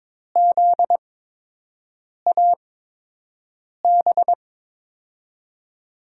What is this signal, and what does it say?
Morse code, heard as a keyed tone, about 22 wpm, 700 Hz. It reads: Z A B
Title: USA Call Signs - 6 Characters - 8x Character Spacing - 22wpm